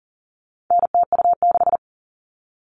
Morse code, heard as a keyed tone, about 40 wpm, 700 Hz. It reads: NTU6
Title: All Cut Numbers - 4-Digit 40wpm